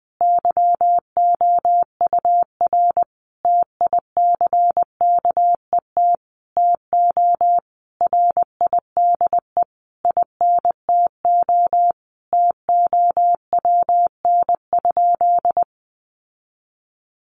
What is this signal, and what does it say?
Morse code, heard as a keyed tone, about 20 wpm, 700 Hz. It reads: YOUR TICKET TO RIDE INTO TOWN?
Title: Sentences from Top 1400 Words 20wpm